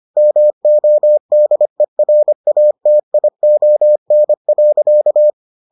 Morse code, heard as a keyed tone, about 25 wpm, 600 Hz. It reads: MODERATION.